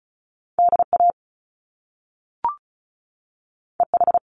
Morse code, heard as a keyed tone, about 36 wpm, 700 Hz. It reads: DA EH
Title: Top Two Letter Combinations in Words 35wpm